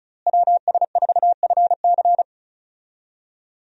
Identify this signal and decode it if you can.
Morse code, heard as a keyed tone, about 36 wpm, 700 Hz. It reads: WS4FC
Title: USA Call Signs - 5 Characters - Repeated 3x - 35wpm